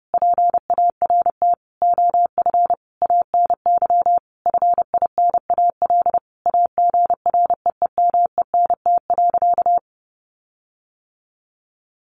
Morse code, heard as a keyed tone, about 30 wpm, 700 Hz. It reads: PART OF ANY FINAL AGREEMENT.
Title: Sentences from Top 1500 Words 30wpm